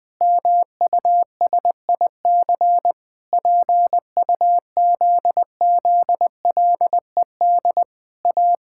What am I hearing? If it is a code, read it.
Morse code, heard as a keyed tone, about 20 wpm, 700 Hz. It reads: MUSIC PUZZLED A
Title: Pangram Challenge v2 - 20wpm